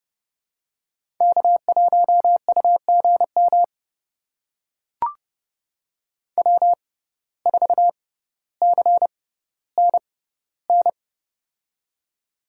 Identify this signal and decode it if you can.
Morse code, heard as a keyed tone, about 30 wpm, spaced as for 15 wpm, 700 Hz. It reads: K1UGM W4CNN
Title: USA Call Signs - 5 Characters - 6x Character Spacing - 30wpm